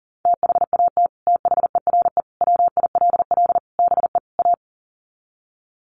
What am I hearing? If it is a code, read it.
Morse code, heard as a keyed tone, about 40 wpm, 700 Hz. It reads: THAT THERE WILL BE A
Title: Sets of 5 Words Encore2 40wpm